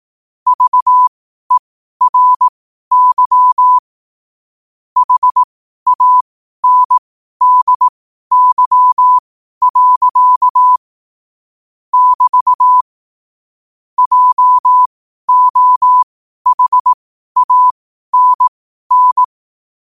Morse code, heard as a keyed tone, about 18 wpm, spaced as for 12 wpm, 1000 Hz. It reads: VERY HANDY. <BT> JOHANN
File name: qotdfiles / 12wpm